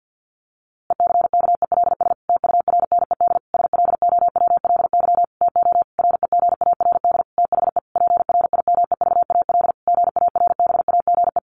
Morse code, heard as a keyed tone, about 50 wpm, 700 Hz. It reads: ECKELS TURNED SLOWLY TO REGARD THE PRIMEVAL GARBAGE
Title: A Sound Of Thunder 50wpm